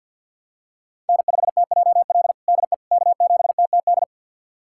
Morse code, heard as a keyed tone, about 50 wpm, 700 Hz. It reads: N5TJL DE K7TTD
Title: USA Callsign DE Callsign - 5 Characters - 50wpm